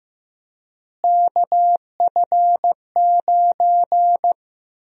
Morse code, heard as a keyed tone, about 16 wpm, 700 Hz. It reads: KF9
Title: USA Call Sign Prefixes - Repeated 2x - 15wpm